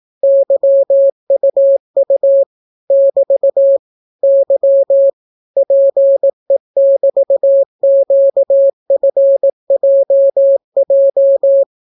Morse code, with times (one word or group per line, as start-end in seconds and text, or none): 0.2-2.4 YUU
2.9-3.8 <BT>
4.2-5.1 Y
5.6-11.6 PE<BT>QFJJ